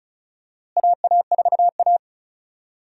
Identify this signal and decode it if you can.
Morse code, heard as a keyed tone, about 34 wpm, 700 Hz. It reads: AA4A
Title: USA Call Signs - 4 Characters - Repeated 2x - 35wpm